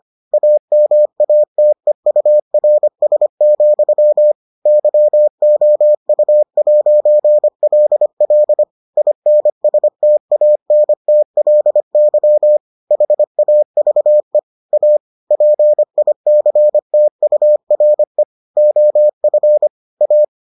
Morse code, heard as a keyed tone, about 25 wpm, 600 Hz. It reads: AMATEURS, YOU'LL INSTANTLY HAVE A PICTURE OF A